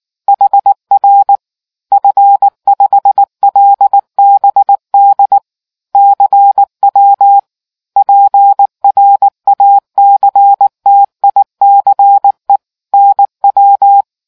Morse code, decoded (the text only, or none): HR F5LBD CW PRACTICE NW